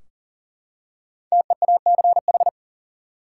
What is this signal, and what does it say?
Morse code, heard as a keyed tone, about 40 wpm, 700 Hz. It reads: TEACH
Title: Five Letter Words - 40wpm